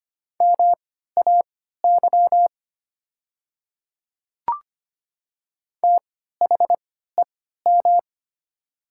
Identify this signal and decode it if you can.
Morse code, heard as a keyed tone, about 25 wpm, spaced as for 14 wpm, 700 Hz. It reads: MAY THEM
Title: 3x Character Spacing - Top 100 Words - 25wpm